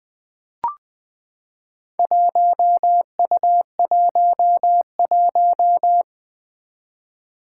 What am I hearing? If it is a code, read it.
Morse code, heard as a keyed tone, about 20 wpm, 700 Hz. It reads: 1U11
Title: Four Challenging Letter-Number Combinations 20wpm